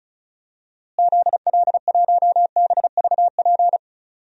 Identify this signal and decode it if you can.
Morse code, heard as a keyed tone, about 36 wpm, 700 Hz. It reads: ZL1BVP
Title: DXCC-100 Call Signs - 6 Characters - Repeated 2x - 35wpm